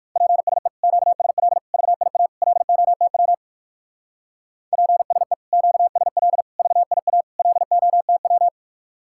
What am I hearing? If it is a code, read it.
Morse code, heard as a keyed tone, about 62 wpm, 700 Hz. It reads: PSE QSL VIA LOTW PSE QSL VIA LOTW